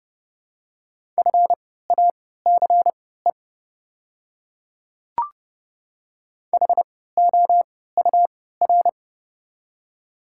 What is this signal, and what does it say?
Morse code, heard as a keyed tone, about 30 wpm, spaced as for 17 wpm, 700 Hz. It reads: FACE HOUR
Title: 3x Character Spacing - Top 500 Words - 30wpm